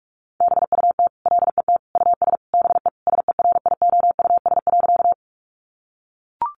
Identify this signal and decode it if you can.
Morse code, heard as a keyed tone, about 44 wpm, 700 Hz. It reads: BUT LET US BE SERIOUS.
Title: Sentences from Top 1100 Words 45wpm